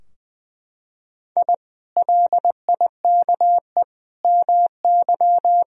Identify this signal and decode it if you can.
Morse code, heard as a keyed tone, about 20 wpm, 700 Hz. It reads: I LIKE MY
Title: Sentences from Top 1000 Words - Review Part 3 - 20wpm